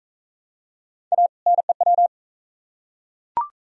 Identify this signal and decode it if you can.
Morse code, heard as a keyed tone, about 40 wpm, 690 Hz. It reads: A NEW